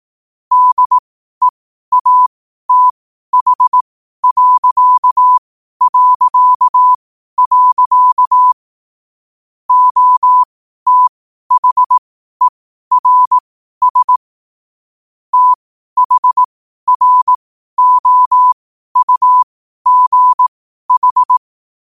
Morse code, heard as a keyed tone, about 18 wpm, spaced as for 12 wpm, 1000 Hz. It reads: DEATH... OTHERS THROUGH
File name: qotdfiles / 12wpm